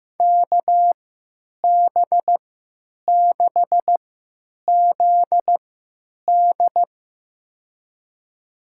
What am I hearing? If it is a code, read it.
Morse code, heard as a keyed tone, about 15 wpm, spaced as for 8 wpm, 700 Hz. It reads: KB6ZD
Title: USA Call Signs - 3x Character Spacing - 15wpm